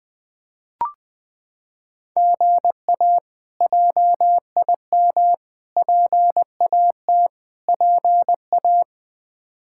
Morse code, heard as a keyed tone, about 20 wpm, 700 Hz. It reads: GA JIM PAT PA
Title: K1USN SST QSO Practice - 20wpm